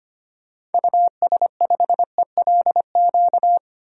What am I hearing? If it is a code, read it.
Morse code, heard as a keyed tone, about 25 wpm, 700 Hz. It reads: US5ELQ
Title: DXCC-100 Call Signs - 6 Characters - Repeated 2x - 25wpm